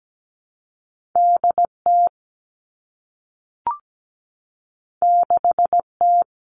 Morse code, heard as a keyed tone, about 17 wpm, 700 Hz. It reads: DT 6T
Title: Two Challenging Letter-Number Combinations 17wpm